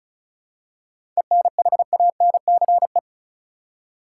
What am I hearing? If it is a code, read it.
Morse code, heard as a keyed tone, about 36 wpm, 700 Hz. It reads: ENHANCE
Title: Seven Letter Words - 35wpm